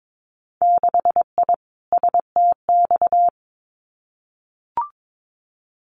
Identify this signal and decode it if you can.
Morse code, heard as a keyed tone, about 22 wpm, 700 Hz. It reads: 6I STX
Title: Winter Field Day Exchanges Encore 22wpm